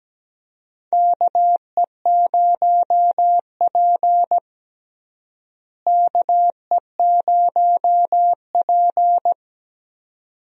Morse code, heard as a keyed tone, about 17 wpm, 700 Hz. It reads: KE0P KE0P
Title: USA Call Signs - 4 Characters - Repeated 2x - 17wpm